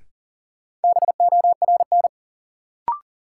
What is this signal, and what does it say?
Morse code, heard as a keyed tone, about 40 wpm, 700 Hz. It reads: BORN
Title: Sentences from Top 1000 Words - New Word Drill - 40wpm